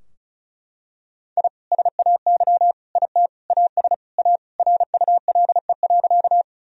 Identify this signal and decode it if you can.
Morse code, heard as a keyed tone, about 36 wpm, 700 Hz. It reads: I SAY IT AS A RULE.
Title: Sentences from Top 1000 Words - Review Part 2 - 35wpm